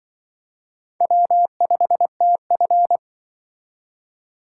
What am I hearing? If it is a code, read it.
Morse code, heard as a keyed tone, about 24 wpm, 700 Hz. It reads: W5TF
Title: Speed Racing - USA Call Signs 20wpm